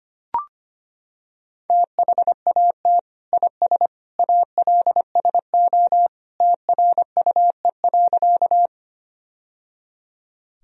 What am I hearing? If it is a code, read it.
Morse code, heard as a keyed tone, about 25 wpm, 700 Hz. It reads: THAT IS ALSO TRUE.
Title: Sentences from Top 1000 Words - Review Part 1 - 25wpm